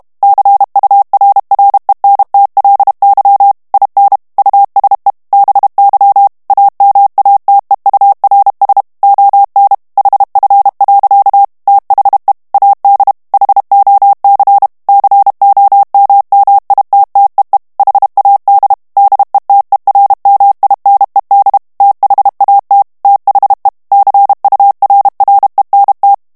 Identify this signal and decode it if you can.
Morse code, heard as a keyed tone, about 32 wpm, 800 Hz. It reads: CURRENTLY IN USE BY AMATEURS ON HF. THE AD HOC COMMITTEE HAD DETERMINED THAT THE CURRENT
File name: ANGLAIS texte clair 3 vitesse 30 mots minute